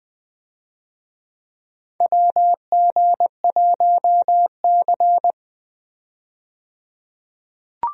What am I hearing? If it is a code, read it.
Morse code, heard as a keyed tone, about 20 wpm, 700 Hz. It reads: WG1C
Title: USA Call Signs - 4 Characters - 20wpm